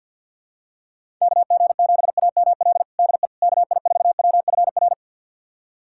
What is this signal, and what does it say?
Morse code, heard as a keyed tone, about 47 wpm, 700 Hz. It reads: KG7AML DE KI4WFR
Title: USA Callsign DE Callsign 50wpm